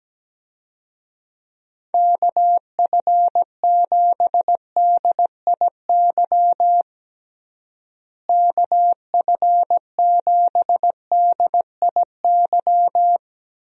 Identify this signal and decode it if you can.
Morse code, heard as a keyed tone, about 17 wpm, 700 Hz. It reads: KF7DIY KF7DIY
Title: USA Call Signs - 6 Characters - Repeated 2x - 17wpm